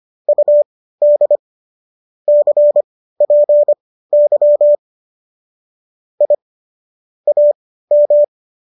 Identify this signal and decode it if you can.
Morse code, heard as a keyed tone, about 25 wpm, spaced as for 15 wpm, 590 Hz. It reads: UD CPY I AM